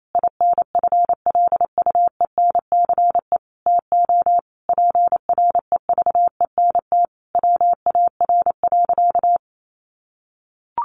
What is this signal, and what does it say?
Morse code, heard as a keyed tone, about 28 wpm, 700 Hz. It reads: INFLUENCE TO PREVENT WAR.
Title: Sentences from Top 1100 Words 28wpm